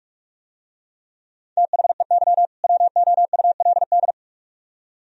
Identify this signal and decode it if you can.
Morse code, heard as a keyed tone, about 44 wpm, 700 Hz. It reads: THEY WOULD